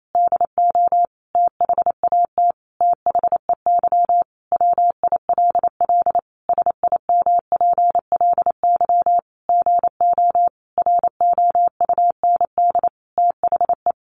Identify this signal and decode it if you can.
Morse code, heard as a keyed tone, about 28 wpm, 700 Hz. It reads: DO THAT THEY WILL SIMPLY GO ROUND THE
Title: Sentences from Top 600 Words 28wpm